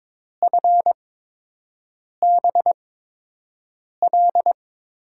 Morse code, heard as a keyed tone, about 22 wpm, spaced as for 6 wpm, 700 Hz. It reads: FBL
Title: USA Call Signs - 6 Characters - 8x Character Spacing - 22wpm